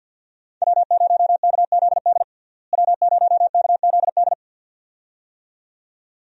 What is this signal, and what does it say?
Morse code, heard as a keyed tone, about 51 wpm, 700 Hz. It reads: W0KZD W0KZD
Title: USA Call Signs - 5 Characters - Repeated 2x - 50wpm